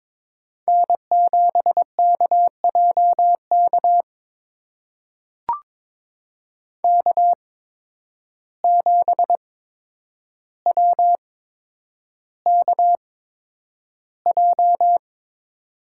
Morse code, heard as a keyed tone, about 22 wpm, spaced as for 9 wpm, 700 Hz. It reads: N7KJK K7WKJ
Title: USA Call Signs - 5 Characters - 8x Character Spacing - 22wpm